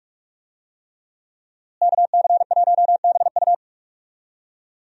Morse code, heard as a keyed tone, about 43 wpm, 700 Hz. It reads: KC1BU
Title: USA Call Signs - 5 Characters - Repeated 3x - 45wpm